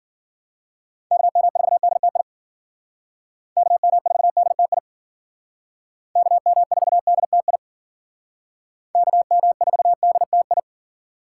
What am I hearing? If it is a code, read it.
Morse code, heard as a keyed tone, about 52 wpm, 700 Hz. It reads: KM4DTI KM4DTI KM4DTI KM4DTI